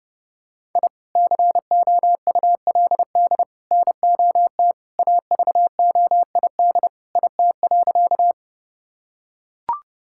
Morse code, heard as a keyed tone, about 30 wpm, 700 Hz. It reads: I COULD NOT AVOID IT.
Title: Sentences from Top 1000 Words - Review Part 6 - 30wpm